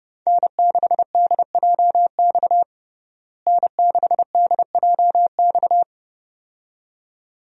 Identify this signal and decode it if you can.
Morse code, heard as a keyed tone, about 32 wpm, 700 Hz. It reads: N6DJX N6DJX
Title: USA Call Signs - 5 Characters - Repeated 2x - 30wpm